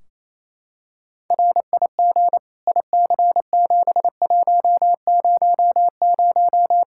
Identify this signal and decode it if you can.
Morse code, heard as a keyed tone, about 28 wpm, 700 Hz. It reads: RIG IC7100
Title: Practice QSO - Exchange 2 - Info - 28wpm